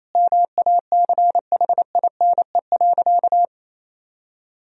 Morse code, heard as a keyed tone, about 27 wpm, 700 Hz. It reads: MACHINE.
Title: Sentences from Top 1000 Words - Review Part 2 - 28wpm